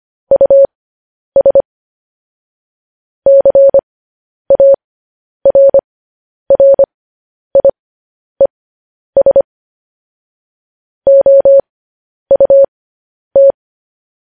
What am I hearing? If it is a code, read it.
Morse code, heard as a keyed tone, about 25 wpm, spaced as for 10 wpm, 550 Hz. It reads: US CARRIES OUT